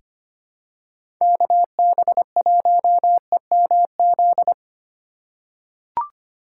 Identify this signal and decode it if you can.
Morse code, heard as a keyed tone, about 25 wpm, 700 Hz. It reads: KB1EMZ
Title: USA Call Signs - 6 Characters - 3x Character Spacing - 25wpm